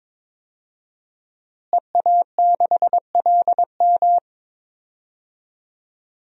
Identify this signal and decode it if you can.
Morse code, heard as a keyed tone, about 22 wpm, 700 Hz. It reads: EA6LM
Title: DXCC-100 Call Signs - 5 Characters - Repeated 3x - 22wpm